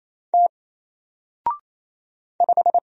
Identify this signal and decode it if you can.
Morse code, heard as a keyed tone, about 27 wpm, 700 Hz. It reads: T 5
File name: Single-Letter-Number-28wpm